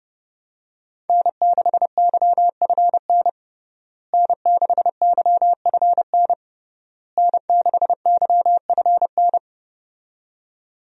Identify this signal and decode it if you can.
Morse code, heard as a keyed tone, about 30 wpm, 700 Hz. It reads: N6YFN N6YFN N6YFN
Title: USA Call Signs - 5 Characters - Repeated 3x - 30wpm